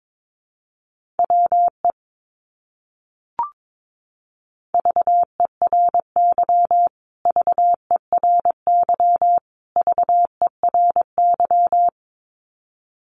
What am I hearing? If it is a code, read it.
Morse code, heard as a keyed tone, about 22 wpm, 700 Hz. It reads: WE VERY VERY VERY
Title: Top 200 Words Repeated 22wpm